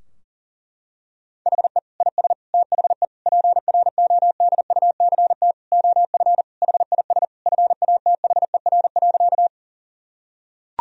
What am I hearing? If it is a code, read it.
Morse code, heard as a keyed tone, about 40 wpm, 700 Hz. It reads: HE IS THE PRODUCT OF HIS FATHER.
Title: Sentences from Top 1000 Words - Review Part 2 - 40wpm